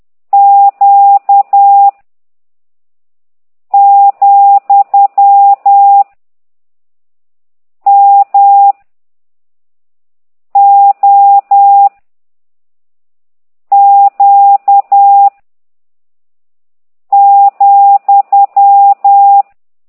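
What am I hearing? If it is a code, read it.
Morse code, heard as a keyed tone, about 10 wpm, 800 Hz. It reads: Q , M O Q ,
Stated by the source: Morse code lesson 5 exercise 10 by CT2IDW, containing only characters M, O, G, Z, Q and comma.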